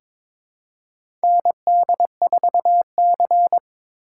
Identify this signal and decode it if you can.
Morse code, heard as a keyed tone, about 22 wpm, 700 Hz. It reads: ND4C
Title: USA Call Signs - 4 Characters - 2x Character Spacing - 22wpm